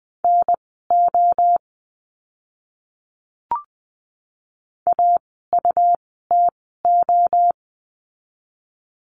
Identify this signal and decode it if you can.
Morse code, heard as a keyed tone, about 20 wpm, spaced as for 14 wpm, 700 Hz. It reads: NO AUTO
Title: 101 Lesson 17f - Letter u - Words 20wpm